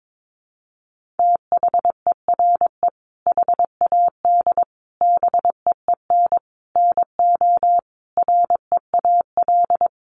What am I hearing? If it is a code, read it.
Morse code, heard as a keyed tone, about 22 wpm, 700 Hz. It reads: THERE HAD BEEN NO REAL
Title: Sentences from Top 1400 Words 22wpm